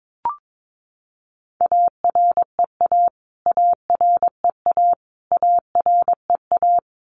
Morse code, heard as a keyed tone, about 22 wpm, 700 Hz. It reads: AREA AREA AREA
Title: Top 500 Words Repeated 22wpm